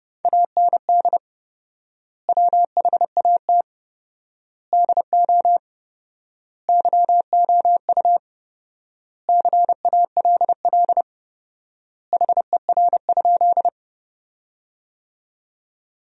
Morse code, heard as a keyed tone, about 30 wpm, 700 Hz. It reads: AND WHAT DO YOU CALL HER?
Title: 4x Word-Spacing - Sentences from Top 100 Words - 30wpm